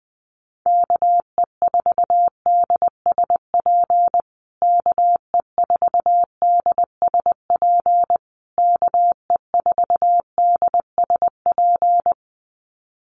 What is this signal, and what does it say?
Morse code, heard as a keyed tone, about 20 wpm, 700 Hz. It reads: KE4DSP KE4DSP KE4DSP
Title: USA Call Signs Repeated 3x 20wpm